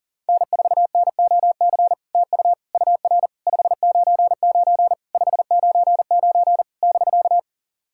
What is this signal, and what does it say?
Morse code, heard as a keyed tone, about 40 wpm, 690 Hz. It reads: N4NOC TU UR 599 599 <BK>
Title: SOTA QSO Practice 40wpm